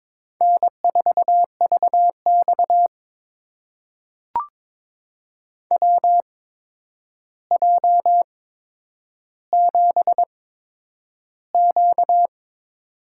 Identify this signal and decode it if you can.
Morse code, heard as a keyed tone, about 22 wpm, spaced as for 9 wpm, 700 Hz. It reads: N4VX WJ7Q
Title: USA Call Signs - 4 Characters - 8x Character Spacing - 22wpm